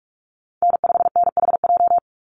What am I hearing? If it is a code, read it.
Morse code, heard as a keyed tone, about 44 wpm, 700 Hz. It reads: N5NHJ
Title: Call Signs Repeated 3x - 45wpm